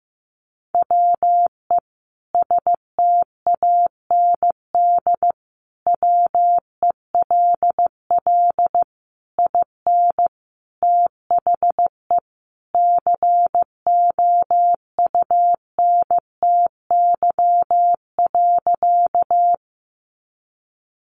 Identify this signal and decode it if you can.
Morse code, heard as a keyed tone, about 15 wpm, 700 Hz. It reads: WE STAND WELL IN THE COUNTY.
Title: Sentences from Top 1200 Words 15wpm